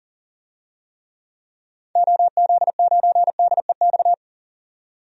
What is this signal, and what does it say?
Morse code, heard as a keyed tone, about 40 wpm, 700 Hz. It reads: OZ9DEX
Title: DXCC-100 Call Signs - 6 Characters - 40wpm